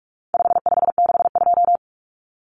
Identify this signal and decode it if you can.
Morse code, heard as a keyed tone, about 44 wpm, 700 Hz. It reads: 5562
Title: Common Cut Numbers - 4-Digit 45wpm